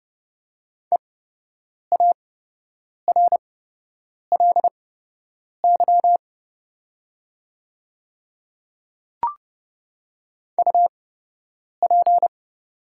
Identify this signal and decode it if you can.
Morse code, heard as a keyed tone, about 30 wpm, spaced as for 8 wpm, 700 Hz. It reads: EARLY UP
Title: 8x Character Spacing - Top 500 Words - 30wpm